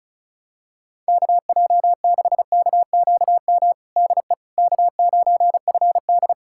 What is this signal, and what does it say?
Morse code, heard as a keyed tone, about 36 wpm, 700 Hz. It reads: KJ6KQM DE K9FD
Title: K1USN SST QSO Practice 35wpm